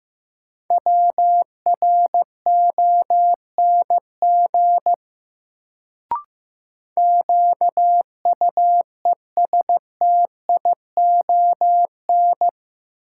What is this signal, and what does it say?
Morse code, heard as a keyed tone, about 15 wpm, 700 Hz. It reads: WRONG QUESTION
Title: Sets of 5 Words Encore - New Word Drill - 15wpm